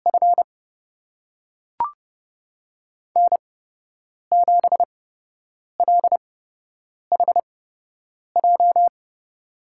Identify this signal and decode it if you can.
Morse code, heard as a keyed tone, about 32 wpm, spaced as for 8 wpm, 700 Hz. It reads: F N7LHJ
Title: USA Call Signs - 5 Characters - 8x Character Spacing - 30wpm